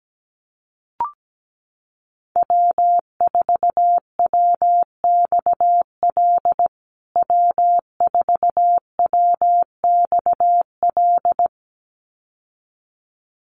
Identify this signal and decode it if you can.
Morse code, heard as a keyed tone, about 17 wpm, 700 Hz. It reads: W4WXL W4WXL
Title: USA Call Signs Repeated 2x 17wpm